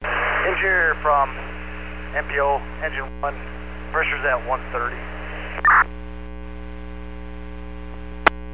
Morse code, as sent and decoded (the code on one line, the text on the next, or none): none